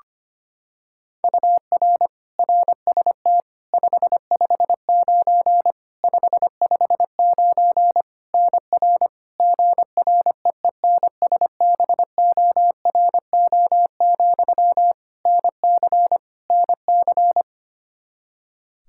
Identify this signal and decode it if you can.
Morse code, heard as a keyed tone, about 25 wpm, 700 Hz. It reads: UR RST 559 559 NR GREENSBORO, NC NC
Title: K1USN SST QSO Practice 25wpm